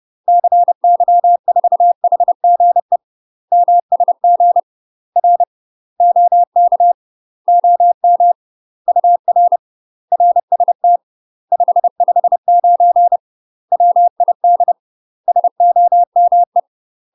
Morse code, heard as a keyed tone, about 30 wpm, 700 Hz. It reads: CY4HGE MSG R OK OM UR RST 559 WID SOME
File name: UK MASTER QSOs TRAINING / 30WPM / UK MASTER QSOs 0002